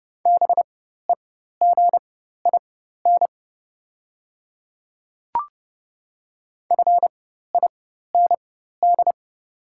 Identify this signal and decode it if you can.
Morse code, heard as a keyed tone, about 31 wpm, spaced as for 14 wpm, 700 Hz. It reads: BEGIN FIND
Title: 4x Character Spacing - Top 300 Words - 30wpm